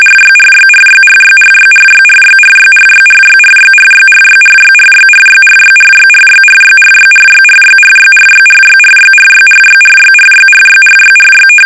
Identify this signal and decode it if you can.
RTTY, about 45.45 baud. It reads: RYRYRYRYRYRYRYRYRYRYRYRYRYRYRYRYRYRYRYRYRYRYRYRYRYRYRYRYRYRYRYRYRYRY
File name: rtty / RTTY Test 850